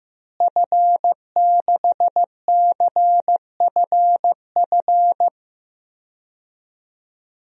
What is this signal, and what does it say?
Morse code, heard as a keyed tone, about 15 wpm, 700 Hz. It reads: F6CFF
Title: DXCC-100 Call Signs - 5 Characters - Repeated 3x - 15wpm